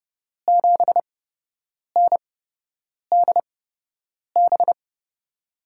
Morse code, heard as a keyed tone, about 31 wpm, spaced as for 8 wpm, 700 Hz. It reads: 7NDB
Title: USA Call Signs - 5 Characters - 8x Character Spacing - 30wpm